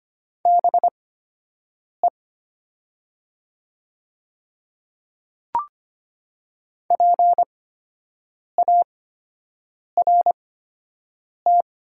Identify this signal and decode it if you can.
Morse code, heard as a keyed tone, about 25 wpm, spaced as for 7 wpm, 700 Hz. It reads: BE PART
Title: 8x Character Spacing - Top 200 Words - 25wpm